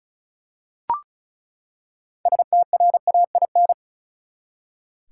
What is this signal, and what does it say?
Morse code, heard as a keyed tone, about 34 wpm, 700 Hz. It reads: STRAIN
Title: Six Letter Words - 35wpm